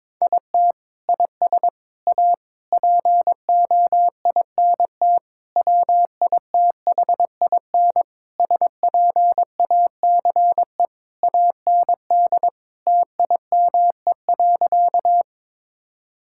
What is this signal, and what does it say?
Morse code, heard as a keyed tone, about 22 wpm, 700 Hz. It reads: IT IS A POINT WITHIN SPACE AND TIME.
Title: Sentences from Top 1000 Words - Review Part 5 - 22wpm